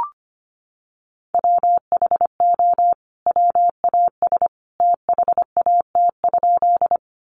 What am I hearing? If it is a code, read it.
Morse code, heard as a keyed tone, about 25 wpm, 700 Hz. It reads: WHO WAS THAT?
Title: Sentences from Top 100 Words 25wpm